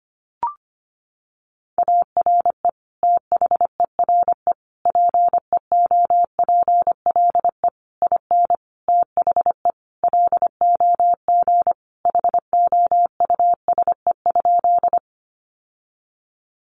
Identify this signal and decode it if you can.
Morse code, heard as a keyed tone, about 25 wpm, 700 Hz. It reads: ARE THERE PEOPLE IN THE LOG HOUSE?
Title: Sentences from Top 1400 Words 25wpm